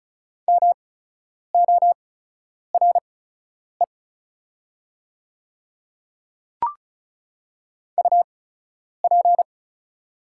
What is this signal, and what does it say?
Morse code, heard as a keyed tone, about 36 wpm, spaced as for 10 wpm, 690 Hz. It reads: MORE UP
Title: 8x Character Spacing - Top 200 Words - 35wpm